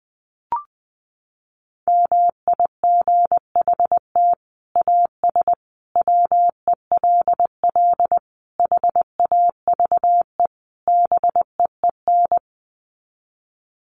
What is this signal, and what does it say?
Morse code, heard as a keyed tone, about 19 wpm, 700 Hz. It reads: MIGHT AS WELL HAVE BEEN
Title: Sets of 5 Words Encore2 20wpm